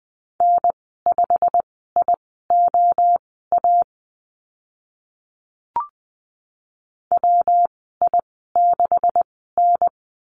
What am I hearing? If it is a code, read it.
Morse code, heard as a keyed tone, about 20 wpm, spaced as for 14 wpm, 700 Hz. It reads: N5IOA WI6N
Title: 173 Lesson 28g - Number 9 - Call Signs 20wpm